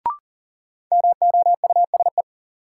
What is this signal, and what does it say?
Morse code, heard as a keyed tone, about 40 wpm, 700 Hz. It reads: MOUSE
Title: Sentences from Top 700 Words - New Word Drill - 40wpm